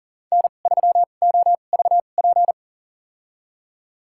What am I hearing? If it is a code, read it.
Morse code, heard as a keyed tone, about 40 wpm, spaced as for 29 wpm, 700 Hz. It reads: N3OVP
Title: USA Call Signs - 5 Characters - 2x Character Spacing - 40wpm